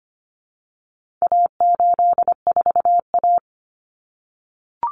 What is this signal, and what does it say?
Morse code, heard as a keyed tone, about 25 wpm, 700 Hz. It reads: A84A
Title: Common Cut Numbers - 4-Digit 25wpm